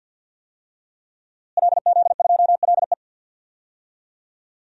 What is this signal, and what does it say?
Morse code, heard as a keyed tone, about 46 wpm, 700 Hz. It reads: LZ2LE
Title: DXCC-100 Call Signs - 5 Characters - Repeated 2x - 50wpm